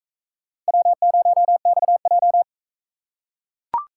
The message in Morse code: W0XJ

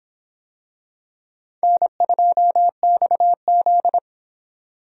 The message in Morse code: N2XZ

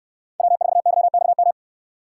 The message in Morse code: W4YXM

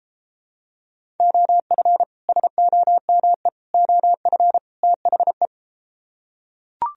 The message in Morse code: OF SOME OF THE

